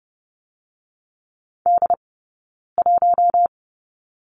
Morse code, heard as a keyed tone, about 30 wpm, 700 Hz. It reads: D 1
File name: Lesson 16d - Letter c - Two Characters 30wpm